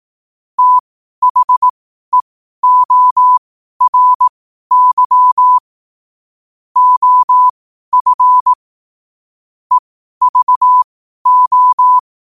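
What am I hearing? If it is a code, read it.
Morse code, heard as a keyed tone, about 18 wpm, spaced as for 12 wpm, 1000 Hz. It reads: THEORY OF EVO
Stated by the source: Morse code at 12 wpm.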